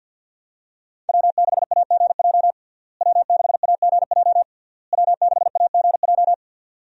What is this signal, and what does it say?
Morse code, heard as a keyed tone, about 51 wpm, 700 Hz. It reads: W6AGJ W6AGJ W6AGJ
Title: USA Call Signs - 5 Characters - Repeated 3x - 50wpm